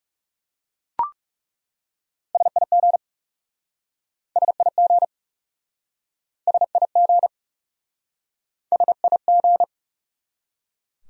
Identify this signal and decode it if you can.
Morse code, heard as a keyed tone, about 40 wpm, 700 Hz. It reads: SIG SIG SIG SIG